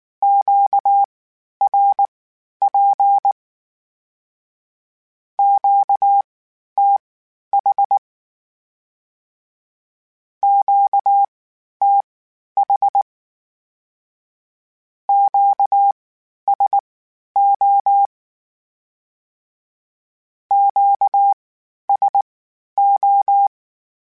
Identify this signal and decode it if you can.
Morse code, heard as a keyed tone, about 19 wpm, spaced as for 11 wpm, 800 Hz. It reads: QRP QTH QTH QSO QSO
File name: (spaced) / Q-Calls Spaced-mp3s / Q Calls Twice sp @18